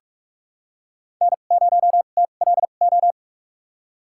Morse code, heard as a keyed tone, about 45 wpm, spaced as for 32 wpm, 700 Hz. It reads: N0TLO
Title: USA Call Signs - 2x Character Spacing - 45wpm